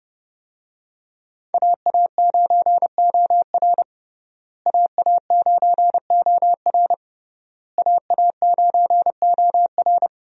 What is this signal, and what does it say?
Morse code, heard as a keyed tone, about 30 wpm, 700 Hz. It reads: AA9OR AA9OR AA9OR
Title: USA Call Signs - 5 Characters - Repeated 3x - 30wpm